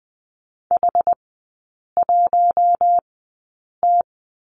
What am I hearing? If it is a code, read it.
Morse code, heard as a keyed tone, about 20 wpm, spaced as for 8 wpm, 700 Hz. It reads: H1T
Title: 184 Lesson 30e - Letter b - Three Characters 20wpm